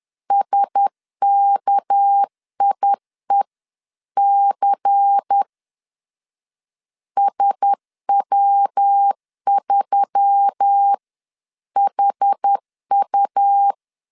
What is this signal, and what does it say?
Morse code, heard as a keyed tone, about 10 wpm, spaced as for 9 wpm, 800 Hz. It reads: SKIEC SW3HU
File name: LEZIONI / 6 PKX0 / 6 PK 10wpm 2436